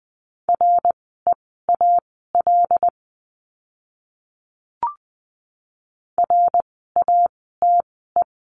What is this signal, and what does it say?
Morse code, heard as a keyed tone, about 20 wpm, spaced as for 14 wpm, 700 Hz. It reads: REAL RATE
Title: 075 Lesson 13f - Letter L - Words 20wpm